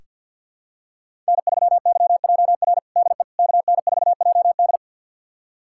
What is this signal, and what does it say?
Morse code, heard as a keyed tone, about 46 wpm, 700 Hz. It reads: N3YJR DE KN4JD
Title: USA Callsign DE Callsign - 5 Characters - 50wpm